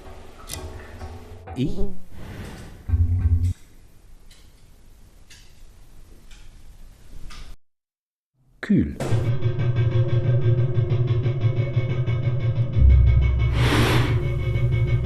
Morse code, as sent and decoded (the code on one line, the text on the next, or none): none